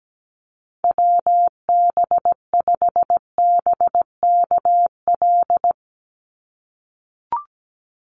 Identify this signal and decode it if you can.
Morse code, heard as a keyed tone, about 17 wpm, 700 Hz. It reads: WB5BKL
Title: Call Signs - 17wpm